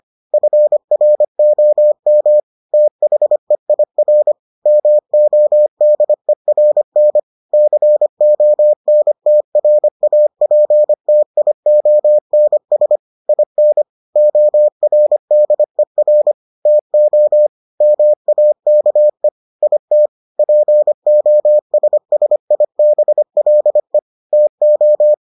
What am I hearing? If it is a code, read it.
Morse code, heard as a keyed tone, about 25 wpm, 600 Hz. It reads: FROM THEIR MODERN CONTRAPTIONS IN ORDER TO MAKE IT POSSIBLE TO